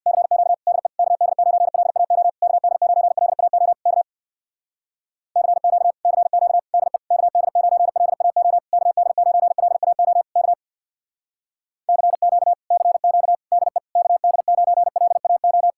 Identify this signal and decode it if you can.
Morse code, heard as a keyed tone, about 65 wpm, 700 Hz. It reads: CQ DE KD9LAO KD9LAO K CQ CQ DE KD9LAO KD9LAO K CQ CQ DE KD9LAO